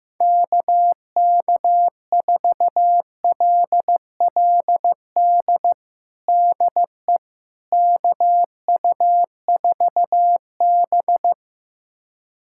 Morse code, heard as a keyed tone, about 15 wpm, 700 Hz. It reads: KK4LLD DE KU4B
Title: USA Callsign DE Callsign 15wpm